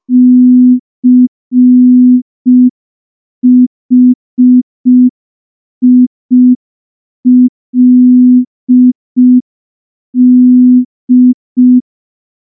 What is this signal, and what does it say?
Morse code, heard as a keyed tone, about 5 wpm, 250 Hz. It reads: CHILD
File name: morse / morse-training-05WPM